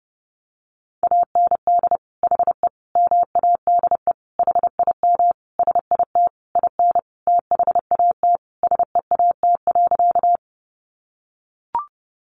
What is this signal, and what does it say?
Morse code, heard as a keyed tone, about 30 wpm, 700 Hz. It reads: AND HE MADE HIM SIT IN THAT SEAT.
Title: Sentences from Top 1500 Words 30wpm